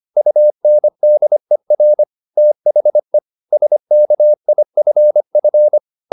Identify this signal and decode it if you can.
Morse code, heard as a keyed tone, about 25 wpm, 600 Hz. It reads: UNDER THE SKIFF